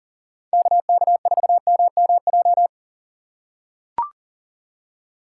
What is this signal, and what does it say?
Morse code, heard as a keyed tone, about 41 wpm, 700 Hz. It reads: KK4MMJ
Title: USA Call Signs - 6 Characters - 4x Character Spacing - 40wpm